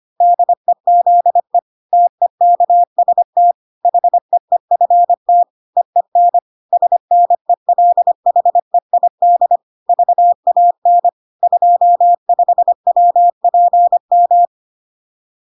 Morse code, heard as a keyed tone, about 25 wpm, 700 Hz. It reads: DEZE TEKST HEEFT EEN SNELHEID VAN 25WPM
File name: 25wpm